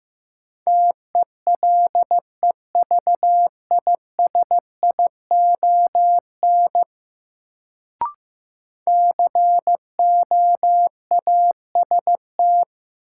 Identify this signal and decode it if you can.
Morse code, heard as a keyed tone, about 16 wpm, 700 Hz. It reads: TELEVISION COAST
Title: Sentences from Top 1000 Words - Review Part 4 - New Word Drill - 15wpm